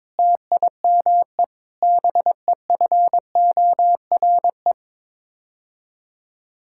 Morse code, heard as a keyed tone, about 22 wpm, 700 Hz. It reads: TIME BEFORE
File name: practice / 22wpm / Sets of 8 Words - 22wpm